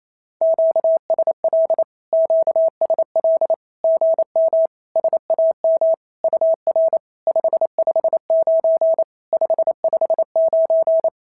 Morse code, heard as a keyed tone, about 28 wpm, 650 Hz. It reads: QSL QSL GM SAM UR 559 559
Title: SOTA QSO Practice 28wpm